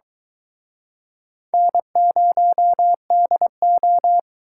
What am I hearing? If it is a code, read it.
Morse code, heard as a keyed tone, about 24 wpm, 700 Hz. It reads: N0DO